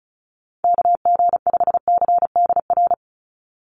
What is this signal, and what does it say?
Morse code, heard as a keyed tone, about 34 wpm, 700 Hz. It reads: KG5CDR
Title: USA Call Signs 35wpm